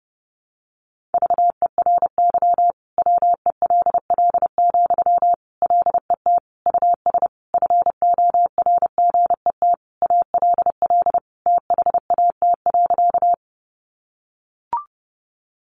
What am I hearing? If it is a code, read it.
Morse code, heard as a keyed tone, about 30 wpm, 700 Hz. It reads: VERY WELL, LET US FORGET ALL THAT.
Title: Sentences from Top 1200 Words 30wpm